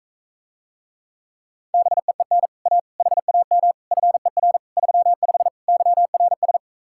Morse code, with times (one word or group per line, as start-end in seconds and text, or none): 1.7-2.5 BEEN
2.7-2.8 A
3.0-3.7 HAM
3.9-4.6 FER
4.8-5.5 35
5.7-6.6 YRS